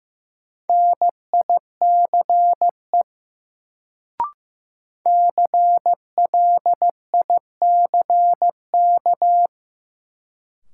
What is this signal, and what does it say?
Morse code, heard as a keyed tone, about 16 wpm, 700 Hz. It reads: NICE CLICK
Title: Sentences from Top 1100 Words - New Word Drill - 15wpm